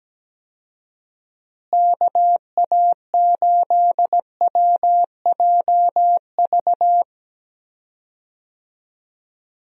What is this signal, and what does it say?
Morse code, heard as a keyed tone, about 17 wpm, 700 Hz. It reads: KA8WJV
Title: USA Call Signs - 6 Characters - Repeated 2x - 17wpm